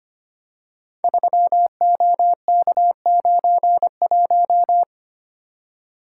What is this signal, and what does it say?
Morse code, heard as a keyed tone, about 25 wpm, 700 Hz. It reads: 3OK91
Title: Code Groups - 5 Letter:Numbers - 25wpm